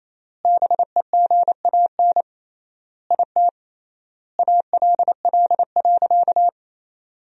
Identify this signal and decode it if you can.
Morse code, heard as a keyed tone, about 27 wpm, 700 Hz. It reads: BEGAN IT ALL.
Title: 3x Word-Spacing - Sentences from Top 300 Words - 28wpm